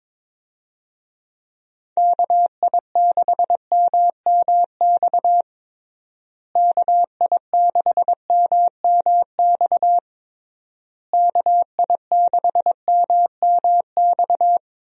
Morse code, heard as a keyed tone, about 22 wpm, 700 Hz. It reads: KI6MMX KI6MMX KI6MMX
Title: USA Call Signs - 6 Characters - Repeated 3x - 22wpm